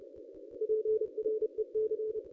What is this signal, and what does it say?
Morse code, heard as a keyed tone, about 30 wpm, 440 Hz. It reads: PREC
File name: contests / ss-cw-k5tr-op-n5rz -1000